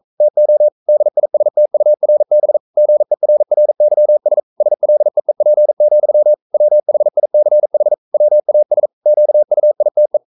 Morse code, heard as a keyed tone, about 42 wpm, 600 Hz. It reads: TO DISTURB GERRYS SLEEP, WHICH WAS QUITE
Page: Romanen Never af Ken Follet på engelsk.